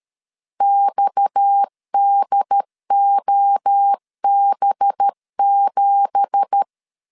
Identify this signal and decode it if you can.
Morse code, heard as a keyed tone, about 13 wpm, 800 Hz. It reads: XDOB7